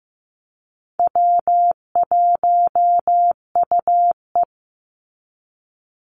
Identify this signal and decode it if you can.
Morse code, heard as a keyed tone, about 15 wpm, 700 Hz. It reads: W1UE
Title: Call Signs Repeated 3x - 15wpm